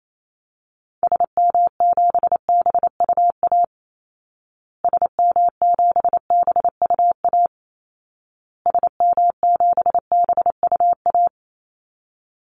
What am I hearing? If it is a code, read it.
Morse code, heard as a keyed tone, about 28 wpm, 700 Hz. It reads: SM7BUA SM7BUA SM7BUA
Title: Call Signs Repeated 2x - 28wpm